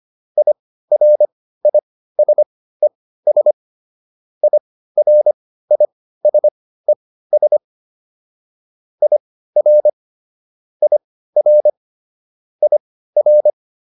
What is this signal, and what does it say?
Morse code, heard as a keyed tone, about 25 wpm, spaced as for 15 wpm, 600 Hz. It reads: IRISES IRISES IR IR IR